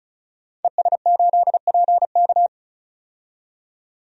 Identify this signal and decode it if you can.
Morse code, heard as a keyed tone, about 36 wpm, 700 Hz. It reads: ES8PK
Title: DXCC-100 Call Signs - 5 Characters - 35wpm